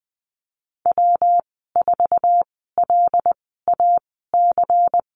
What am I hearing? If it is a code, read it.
Morse code, heard as a keyed tone, about 20 wpm, spaced as for 15 wpm, 700 Hz. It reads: W4LAC
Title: 161 Lesson 26g - Letter g - Call Signs 20wpm